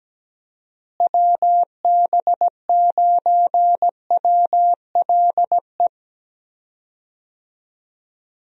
Morse code, heard as a keyed tone, about 17 wpm, 700 Hz. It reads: WB9WLE
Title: USA Call Signs - 6 Characters - Repeated 3x - 17wpm